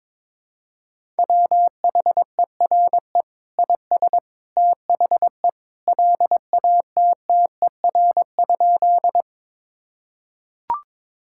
Morse code, heard as a keyed tone, about 21 wpm, 700 Hz. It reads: WHERE IS THE LATTER?
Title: Sentences from Top 1600 Words 22wpm